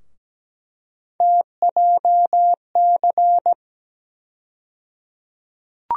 Morse code, heard as a keyed tone, about 17 wpm, 700 Hz. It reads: TJC
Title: USA Call Sign Postfixes - 3 Letters 17wpm